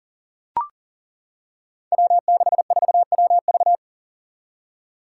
Morse code, heard as a keyed tone, about 40 wpm, 700 Hz. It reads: W64WV
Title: Five Challenging Letter-Number Combinations 40wpm